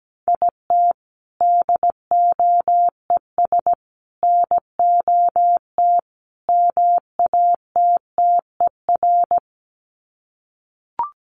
Morse code, heard as a keyed tone, about 17 wpm, 700 Hz. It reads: IT DOES NOT MATTER
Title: Sets of 4 Words 17wpm